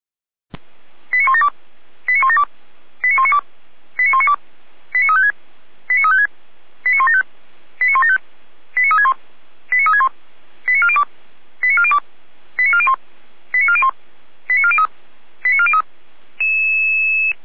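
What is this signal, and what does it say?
Five-tone selective call (ZVEI), heard as quick runs of five tones, tones of about 70 ms. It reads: 89172 89172 89192 89192 89356 89356 89175 89175 89361 89361 89402 89402 89401 89401 89403 89403
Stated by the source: Five-tone ZVEI selective call 89340 received by SDR.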